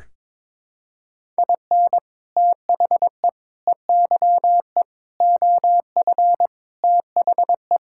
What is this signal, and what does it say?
Morse code, heard as a keyed tone, about 22 wpm, 700 Hz. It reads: IN THE EYE OF THE
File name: practice / 22wpm / Sets of 6 Words - 22wpm